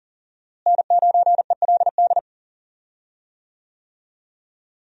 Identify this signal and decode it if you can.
Morse code, heard as a keyed tone, about 40 wpm, 700 Hz. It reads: N9ELD
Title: USA Call Signs - 5 Characters - Repeated 2x - 40wpm